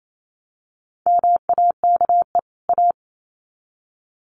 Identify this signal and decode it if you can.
Morse code, heard as a keyed tone, about 28 wpm, 700 Hz. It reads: MAKE A
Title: Sets of 2 Words 28wpm